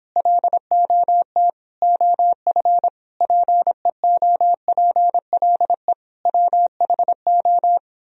Morse code, heard as a keyed tone, about 31 wpm, 700 Hz. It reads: LOT OF PEOPLE WHO